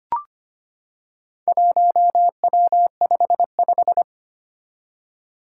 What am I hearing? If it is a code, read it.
Morse code, heard as a keyed tone, about 25 wpm, 700 Hz. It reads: 1W55
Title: Four Challenging Letter-Number Combinations 25wpm